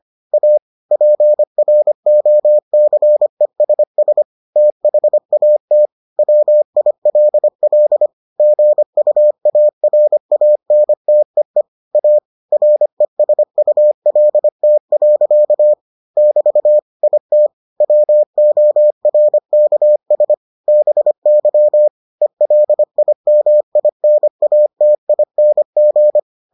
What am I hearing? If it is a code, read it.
Morse code, heard as a keyed tone, about 25 wpm, 600 Hz. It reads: A PROCESS THAT WILL GUARANTEE A RESULT. <BT> IT WORKS BY ELIMINATING